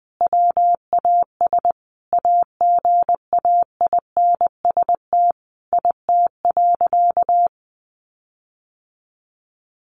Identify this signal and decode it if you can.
Morse code, heard as a keyed tone, about 19 wpm, 700 Hz. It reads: WAS AGAINST IT.
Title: Sentences from Top 1500 Words 20wpm